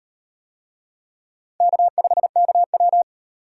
Morse code, heard as a keyed tone, about 38 wpm, 700 Hz. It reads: K5KW